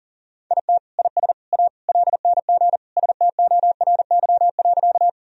Morse code, heard as a keyed tone, about 37 wpm, 700 Hz. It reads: IT IS A LNG STORY.
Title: Sentences from Top 300 Words and Abbreviations 40wpm